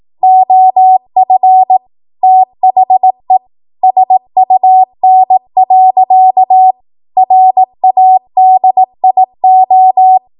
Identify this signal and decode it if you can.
Morse code, heard as a keyed tone, about 18 wpm, 750 Hz. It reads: OF THE SUN. RADIO
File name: Morse / 18WPM